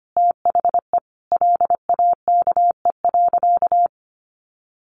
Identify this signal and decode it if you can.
Morse code, heard as a keyed tone, about 25 wpm, 700 Hz. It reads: THE LAKE.
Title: Sentences from Top 1300 Words 25wpm